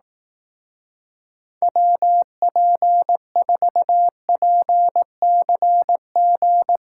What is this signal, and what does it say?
Morse code, heard as a keyed tone, about 18 wpm, 700 Hz. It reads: WP4PCG
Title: Speed Racing - USA 2x3 Call Signs 15wpm